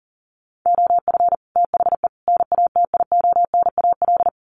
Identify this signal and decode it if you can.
Morse code, heard as a keyed tone, about 40 wpm, 700 Hz. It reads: OF THE NATIONAL
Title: Sets of 3 Words 40wpm